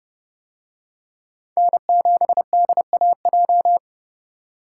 Morse code, heard as a keyed tone, about 29 wpm, 700 Hz. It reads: N7DAJ